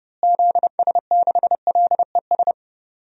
Morse code, heard as a keyed tone, about 30 wpm, 700 Hz. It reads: ZS6LES
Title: DXCC-100 Call Signs - 6 Characters - Repeated 3x - 30wpm